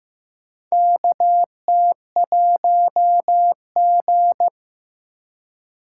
Morse code, heard as a keyed tone, about 15 wpm, 700 Hz. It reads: KT1G
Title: USA Call Signs - 4 Characters - 3x Character Spacing - 15wpm